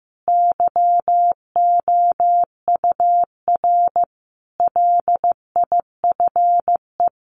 Morse code, heard as a keyed tone, about 14 wpm, 700 Hz. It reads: YOUR LIFE
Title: Sentences from Top 1500 Words 15wpm